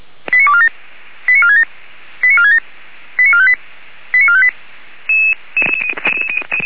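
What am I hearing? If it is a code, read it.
Five-tone selective call (ZVEI), heard as quick runs of five tones, tones of about 70 ms. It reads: 89247 89467 89467 89458 89458
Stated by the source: ZVEI five-tone selective call received on an SDR.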